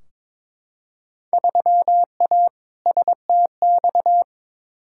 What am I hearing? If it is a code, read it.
Morse code, heard as a keyed tone, about 21 wpm, 700 Hz. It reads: 3A STX
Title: Field Day Exchanges - 22wpm